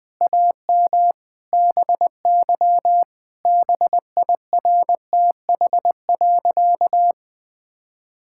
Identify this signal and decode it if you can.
Morse code, heard as a keyed tone, about 20 wpm, 700 Hz. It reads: AM BY BIRTH.
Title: Sentences from Top 1000 Words - Review Part 5 - 20wpm